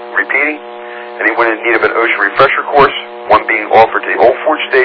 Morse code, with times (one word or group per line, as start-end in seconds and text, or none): none